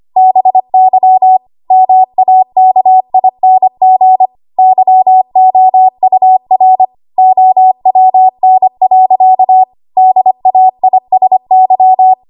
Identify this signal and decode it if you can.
Morse code, heard as a keyed tone, about 25 wpm, 750 Hz. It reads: BY MAKING YOUR OWN. DAISY
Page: W1AW Web Code Practice Files